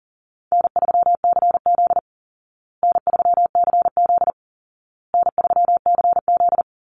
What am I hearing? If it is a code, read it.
Morse code, heard as a keyed tone, about 37 wpm, 700 Hz. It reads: N3CZ N3CZ N3CZ
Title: Call Signs Repeated 2x - 40wpm